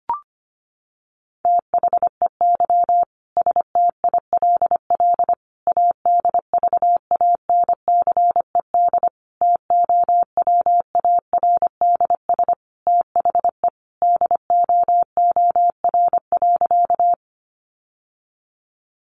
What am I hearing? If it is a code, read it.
Morse code, heard as a keyed tone, about 25 wpm, 700 Hz. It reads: THEY STILL ADVANCED TOWARDS THE DOOR.
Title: Sentences from Top 1200 Words 25wpm